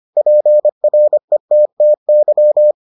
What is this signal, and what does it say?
Morse code, heard as a keyed tone, about 25 wpm, 600 Hz. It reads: PRETTY